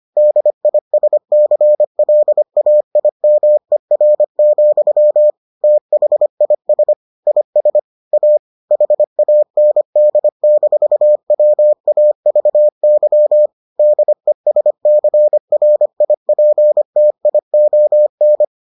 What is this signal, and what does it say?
Morse code, heard as a keyed tone, about 25 wpm, 600 Hz. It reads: DISCLAIMER, THIS IS A HAND-WAVY DESCRIPTION